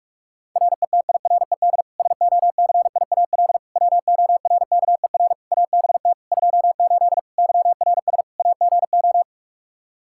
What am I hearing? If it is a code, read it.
Morse code, heard as a keyed tone, about 46 wpm, 700 Hz. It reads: RETIRED SOCIAL WORKER ABT 28 YRS AGO
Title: K1USN SST QSO Practice 45wpm